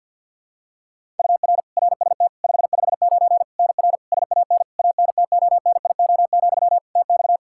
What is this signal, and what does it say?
Morse code, heard as a keyed tone, about 52 wpm, 700 Hz. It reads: UR RST 559 NR SAN ANTONIO, TX
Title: K1USN SST QSO Practice 50wpm